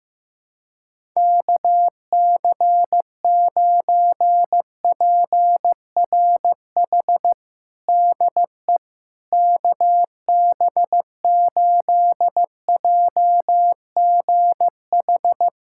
Morse code, heard as a keyed tone, about 16 wpm, 700 Hz. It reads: KC9PRH DE KB8JGH
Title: USA Callsign DE Callsign - 6 Characters - 15wpm